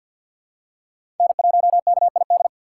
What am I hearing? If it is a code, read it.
Morse code, heard as a keyed tone, about 52 wpm, 700 Hz. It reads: N1KID
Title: USA Call Signs - 5 Characters - 3x Character Spacing - 50wpm